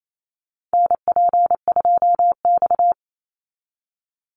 Morse code, heard as a keyed tone, about 28 wpm, 700 Hz. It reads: NP2X
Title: Call Signs Repeated 2x - 28wpm